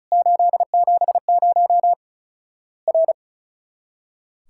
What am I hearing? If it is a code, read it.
Morse code, heard as a keyed tone, about 36 wpm, 700 Hz. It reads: 870 R
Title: Worked All Europe DX Contest - QTC Exchange - 35wpm